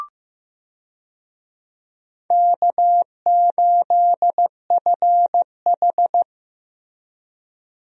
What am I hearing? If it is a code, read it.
Morse code, heard as a keyed tone, about 15 wpm, 700 Hz. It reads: K8FH
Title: USA Call Signs - 4 Characters - Repeated 3x - 15wpm